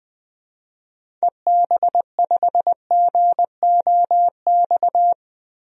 Morse code, heard as a keyed tone, about 20 wpm, 700 Hz. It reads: EB5GOX
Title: DXCC-100 Call Signs - 6 Characters - Repeated 3x - 20wpm